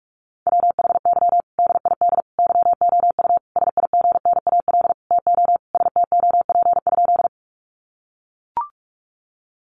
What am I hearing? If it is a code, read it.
Morse code, heard as a keyed tone, about 44 wpm, 700 Hz. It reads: WHY DID YOU SIGNAL TO STOP?
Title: Sentences from Top 1400 Words 45wpm